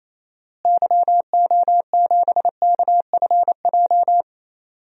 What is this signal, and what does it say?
Morse code, heard as a keyed tone, about 28 wpm, 700 Hz. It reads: YO7KFJ
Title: DXCC-100 Call Signs - 6 Characters - Repeated 2x - 28wpm